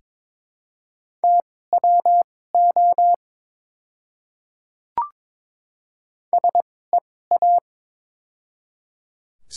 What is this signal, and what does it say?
Morse code, heard as a keyed tone, about 22 wpm, spaced as for 16 wpm, 700 Hz. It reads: TWO SEA
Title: 2x Character Spacing - Top 300 Words - 22wpm